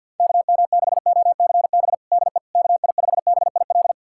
Morse code, heard as a keyed tone, about 52 wpm, 690 Hz. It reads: KM6OCB DE KI5BIL
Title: USA Callsign DE Callsign - 6 Characters - 50wpm